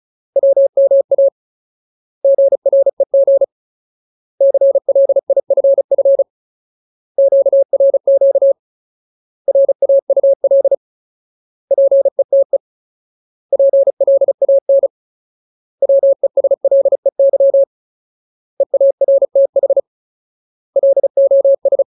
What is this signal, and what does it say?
Morse code, heard as a keyed tone, about 34 wpm, 550 Hz. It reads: WMA GREG CLIFF QRQ RAUL PETE PLAN WESLEY EARTH LOS
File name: Ultimate Word List / 35WPM Ultimate